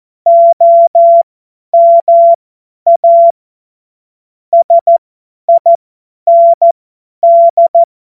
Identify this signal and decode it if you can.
Morse code, heard as a keyed tone, about 14 wpm, spaced as for 10 wpm, 680 Hz. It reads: OMA SIND
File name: Morsekurs / Witze / Witz 12. Oktober 2021, 50 F70